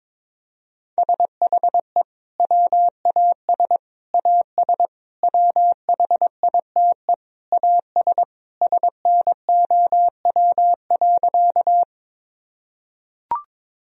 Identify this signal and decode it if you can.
Morse code, heard as a keyed tone, about 22 wpm, 700 Hz. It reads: SHE WAS AS WHITE AS SNOW.
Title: Sentences from Top 1000 Words - Review Part 3 - 22wpm